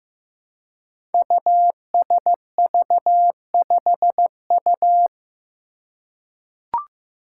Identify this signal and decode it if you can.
Morse code, heard as a keyed tone, about 15 wpm, 700 Hz. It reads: USV5U
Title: Five Challenging Letter-Number Combinations 15wpm